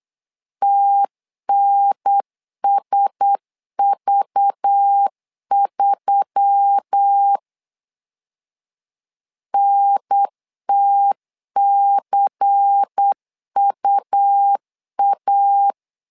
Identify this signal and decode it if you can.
Morse code, heard as a keyed tone, about 8 wpm, 800 Hz. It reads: TNSV3 NTCUA
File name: LEZIONI / 5 LBVC / 5 LBVC 8wpm 2935